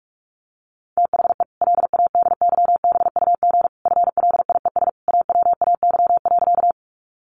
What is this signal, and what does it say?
Morse code, heard as a keyed tone, about 44 wpm, 700 Hz. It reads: THE LADYBUG FLIES AWAY.
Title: Additional Easy Phrases 45wpm